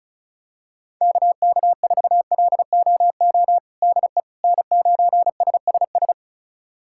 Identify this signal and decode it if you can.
Morse code, heard as a keyed tone, about 36 wpm, 700 Hz. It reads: KK4LOO DE N9SSS
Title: USA Callsign DE Callsign 35wpm